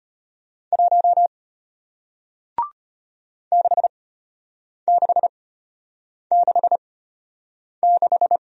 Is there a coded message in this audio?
Yes: Morse code from 0.7 to 8.4 s, heard as a keyed tone, about 33 wpm, spaced as for 8 wpm, 700 Hz. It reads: 1 6666